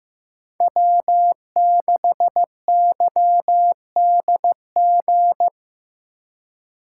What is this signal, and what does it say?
Morse code, heard as a keyed tone, about 17 wpm, 700 Hz. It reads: W6YDG
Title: Speed Racing - USA 1x3 Call Signs 15wpm